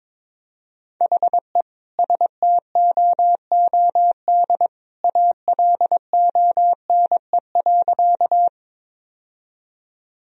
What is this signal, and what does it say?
Morse code, heard as a keyed tone, about 22 wpm, 700 Hz. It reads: HE STOOD ALONE.
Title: Sentences from Top 1000 Words - Review Part 2 - 22wpm